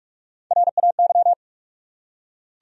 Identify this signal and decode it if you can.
Morse code, heard as a keyed tone, about 46 wpm, 700 Hz. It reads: RAY
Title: USA Common Names - Max 3 Letters - 45wpm